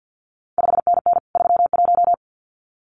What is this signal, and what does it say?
Morse code, heard as a keyed tone, about 50 wpm, 700 Hz. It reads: 5NN 31
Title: CQ WW DX Contest Exchange - 50wpm